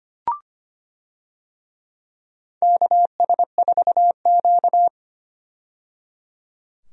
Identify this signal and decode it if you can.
Morse code, heard as a keyed tone, about 25 wpm, 700 Hz. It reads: KS4Q
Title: USA Call Signs - 4 Characters - 25wpm